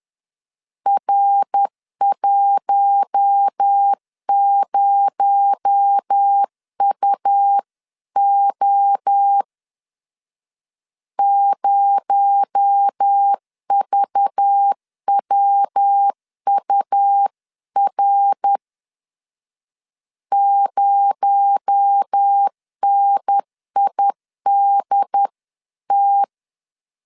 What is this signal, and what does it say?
Morse code, heard as a keyed tone, about 10 wpm, 800 Hz. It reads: R10UO 0VWUR 0NIDT
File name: LEZIONI / 6 PKX0 / 6 PKX0 10wpm 2707